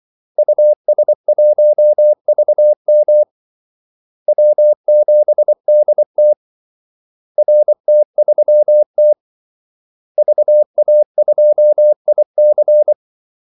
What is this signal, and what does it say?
Morse code, heard as a keyed tone, about 24 wpm, 600 Hz. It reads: US1VM W7DT RT3T VA2IC
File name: Callsigns-24wpm